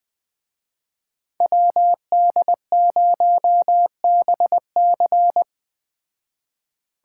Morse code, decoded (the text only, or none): WD0BC